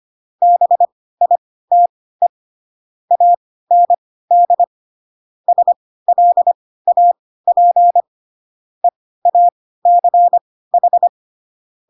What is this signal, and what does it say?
Morse code, heard as a keyed tone, about 25 wpm, spaced as for 16 wpm, 700 Hz. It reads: BITE AND SLAP EACH
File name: Pinocchio / Pinocchio ch2 10 25 ik1hge